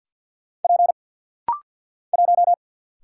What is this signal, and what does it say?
Morse code, heard as a keyed tone, about 50 wpm, 700 Hz. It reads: P 1
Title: Single Letter-Number - Rapid-Fire 50wpm